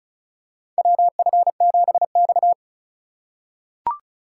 Morse code, heard as a keyed tone, about 34 wpm, 700 Hz. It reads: WF7X
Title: USA Call Signs - 4 Characters - 2x Character Spacing - 35wpm